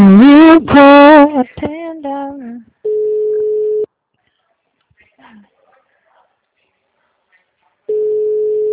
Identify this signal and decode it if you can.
Ringback tone (Europe, 425 Hz), with speech.
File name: out- -2011- - - .79